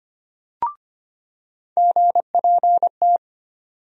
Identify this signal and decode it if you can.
Morse code, heard as a keyed tone, about 25 wpm, 700 Hz. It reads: GPT
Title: USA Call Sign Postfixes - 3 Letters 25wpm